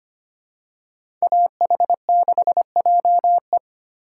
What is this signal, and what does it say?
Morse code, heard as a keyed tone, about 25 wpm, 700 Hz. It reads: AH6JE
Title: USA Call Signs - 5 Characters - 3x Character Spacing - 25wpm